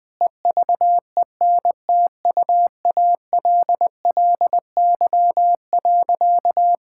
Morse code, heard as a keyed tone, about 21 wpm, 700 Hz. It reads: EVENTUALLY.
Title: Sentences with 8-Word Phrases - 20wpm